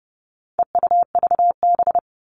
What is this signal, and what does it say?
Morse code, heard as a keyed tone, about 30 wpm, 700 Hz. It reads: EUVB
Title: All Cut Numbers - 4-Digit 30wpm